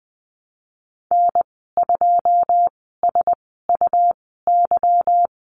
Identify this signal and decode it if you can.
Morse code, heard as a keyed tone, about 20 wpm, spaced as for 14 wpm, 700 Hz. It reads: N2SUY
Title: 161 Lesson 26g - Letter g - Call Signs 20wpm